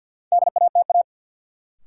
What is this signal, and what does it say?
Morse code, heard as a keyed tone, about 53 wpm, 700 Hz. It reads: DATA
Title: Sentences from Top 1000 Words - Review Part 3 - New Word Drill - 50wpm